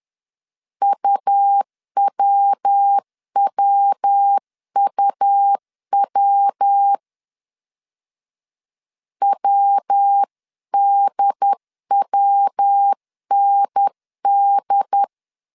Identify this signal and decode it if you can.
Morse code, heard as a keyed tone, about 10 wpm, 800 Hz. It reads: UWWUW WDWND
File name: 10wpm